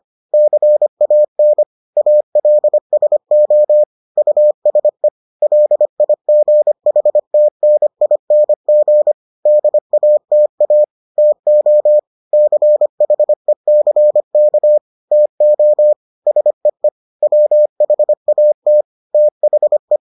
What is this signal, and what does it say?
Morse code, heard as a keyed tone, about 25 wpm, 600 Hz. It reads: CAN ALSO USE LIGHTNING DATA TO CHECK TO SEE WHAT THE